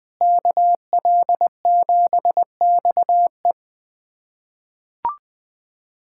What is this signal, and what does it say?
Morse code, heard as a keyed tone, about 21 wpm, 700 Hz. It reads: KL7XE
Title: USA Callsign DE Callsign 20wpm